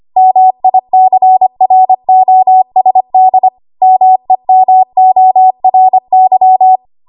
Morse code, heard as a keyed tone, about 25 wpm, 750 Hz. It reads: MICROSD MEMORY
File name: Archive / 25 WPM / 25WPM